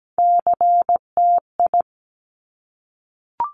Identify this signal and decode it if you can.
Morse code, heard as a keyed tone, about 17 wpm, 700 Hz. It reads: CTI
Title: Top Three Letter Combinations in Words 17wpm